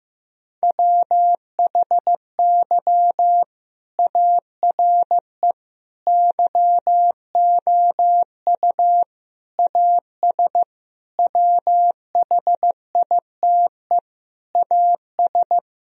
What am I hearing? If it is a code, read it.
Morse code, heard as a keyed tone, about 16 wpm, 700 Hz. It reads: WHY ARE YOU AS WHITE AS
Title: Sentences from Top 1000 Words - Review Part 3 - 15wpm